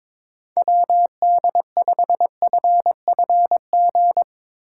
Morse code, heard as a keyed tone, about 21 wpm, 700 Hz. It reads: WD5FFG
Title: USA Call Signs - 6 Characters - 2x Character Spacing - 22wpm